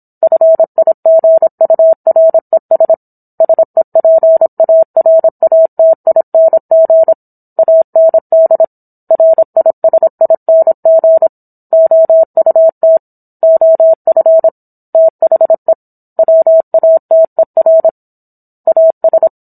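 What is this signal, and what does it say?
Morse code, heard as a keyed tone, about 27 wpm, 650 Hz. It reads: FIGURES SEPARATING AND RISING OUT OF THE WATER AS
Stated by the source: Morse code practice of a book chapter at 26 WPM.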